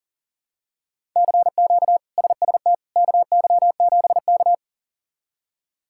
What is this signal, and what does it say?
Morse code, heard as a keyed tone, about 40 wpm, 700 Hz. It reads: CQ SST KY7K
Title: K1USN SST QSO Practice - 40wpm